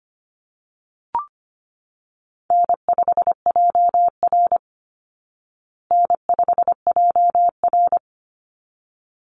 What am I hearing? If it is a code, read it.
Morse code, heard as a keyed tone, about 25 wpm, 700 Hz. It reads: N5JR N5JR
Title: Call Signs Repeated 3x - 25wpm